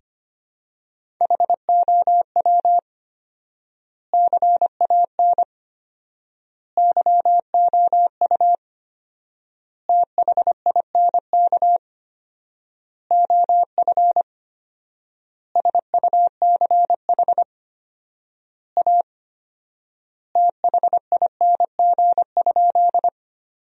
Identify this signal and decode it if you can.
Morse code, heard as a keyed tone, about 25 wpm, 700 Hz. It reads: HOW CAN YOU THINK OF SUCH A THING?
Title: 4x Word-Spacing - Sentences from Top 200 Words - 25wpm